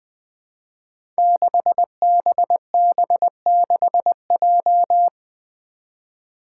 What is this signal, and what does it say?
Morse code, heard as a keyed tone, about 20 wpm, 700 Hz. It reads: 6BB6J
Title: Five Challenging Letter-Number Combinations 20wpm